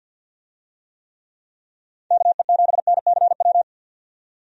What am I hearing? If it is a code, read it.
Morse code, heard as a keyed tone, about 51 wpm, 700 Hz. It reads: KE7NCW
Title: USA Call Signs - 6 Characters - 50wpm